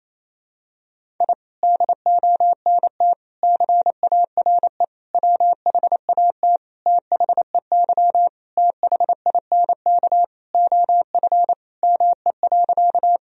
Morse code, heard as a keyed tone, about 27 wpm, 700 Hz. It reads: I DONT CARE WHAT THEY THINK OF ME.
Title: Sentences from Top 1000 Words - Review Part 1 - 28wpm